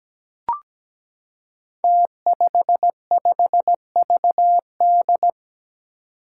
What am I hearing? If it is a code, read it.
Morse code, heard as a keyed tone, about 17 wpm, 700 Hz. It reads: T55VD
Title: Five Challenging Letter-Number Combinations 17wpm